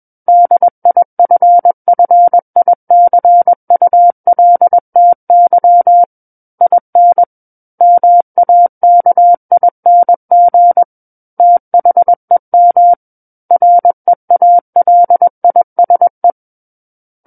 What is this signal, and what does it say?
Morse code, heard as a keyed tone, about 21 wpm, 700 Hz. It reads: DIFFICULTY IN MAKING THEM REALISE
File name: Book One / 12-Chapter 12 - 21 WPM